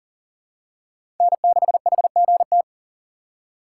Morse code, heard as a keyed tone, about 37 wpm, 690 Hz. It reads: N6HGT
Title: USA Call Signs - 6x Character Spacing - 40wpm